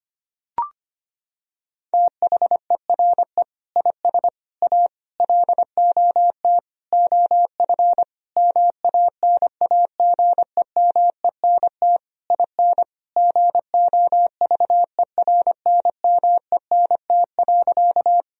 Morse code, heard as a keyed tone, about 25 wpm, 700 Hz. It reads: THERE IS A LOT OF MANAGEMENT IN GOVERNMENT.
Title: Sentences from Top 1000 Words - Review Part 4 - 25wpm